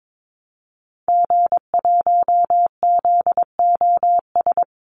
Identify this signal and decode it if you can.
Morse code, heard as a keyed tone, about 22 wpm, 700 Hz. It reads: G1ZOS
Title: Call Signs Repeated 2x - 22wpm